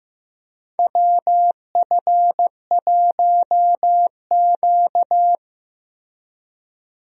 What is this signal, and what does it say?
Morse code, heard as a keyed tone, about 15 wpm, 700 Hz. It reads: WF1Q
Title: USA Call Signs - 4 Characters - Repeated 3x - 15wpm